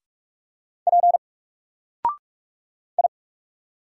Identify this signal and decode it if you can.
Morse code, heard as a keyed tone, about 44 wpm, 700 Hz. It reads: P I
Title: Single Character (Letter:Number:Punctuation:Prosign) 45wpm